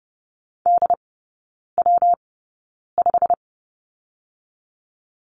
Morse code, heard as a keyed tone, about 30 wpm, spaced as for 9 wpm, 700 Hz. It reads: DW5
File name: Lesson 19e - Letter w - Three Characters 30wpm